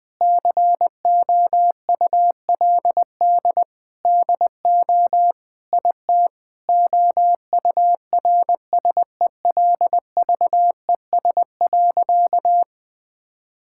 Morse code, heard as a keyed tone, about 20 wpm, 700 Hz. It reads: COULD DO IT OURSELVES.
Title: Sentences from Top 1600 Words 20wpm